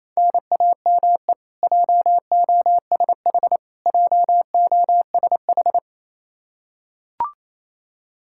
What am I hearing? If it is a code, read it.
Morse code, heard as a keyed tone, about 28 wpm, 700 Hz. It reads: NAME JOSH JOSH
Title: K1USN SST QSO Practice 28wpm